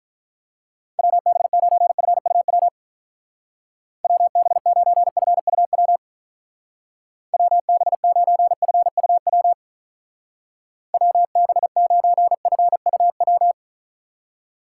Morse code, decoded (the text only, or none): WB9FUW WB9FUW WB9FUW WB9FUW